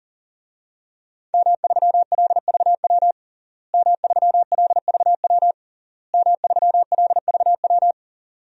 Morse code, heard as a keyed tone, about 40 wpm, 700 Hz. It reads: M3LVW M3LVW M3LVW
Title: DXCC-100 Call Signs - 5 Characters - Repeated 3x - 40wpm